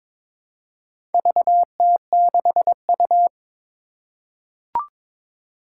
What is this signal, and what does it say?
Morse code, heard as a keyed tone, about 22 wpm, 700 Hz. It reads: VT6U
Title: Four Challenging Letter-Number Combinations 22wpm